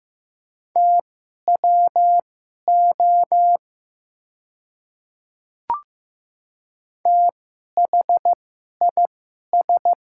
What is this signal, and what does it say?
Morse code, heard as a keyed tone, about 15 wpm, spaced as for 11 wpm, 700 Hz. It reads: TWO THIS
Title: 2x Character Spacing - Top 100 Words - 15wpm